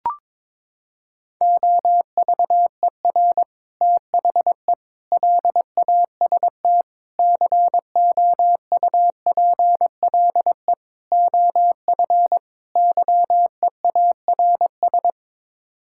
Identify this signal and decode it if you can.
Morse code, heard as a keyed tone, about 22 wpm, 700 Hz. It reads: OVER THE LAST COUPLE OF YEARS
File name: practice / 22wpm / Sets of 6 Words - 22wpm